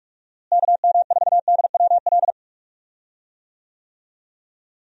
Morse code, heard as a keyed tone, about 41 wpm, 700 Hz. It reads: KM4DWL
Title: USA Call Signs - 6 Characters - Repeated 2x - 45wpm